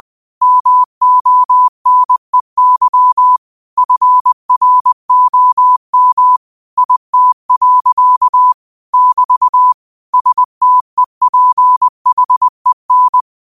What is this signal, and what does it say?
Morse code, heard as a keyed tone, about 20 wpm, 1000 Hz. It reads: MONEY FROM IT. <BT> STEPHEN
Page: Daily quotes in Morse Code at 20 Words per Minute. Use this to improve your code copying skill.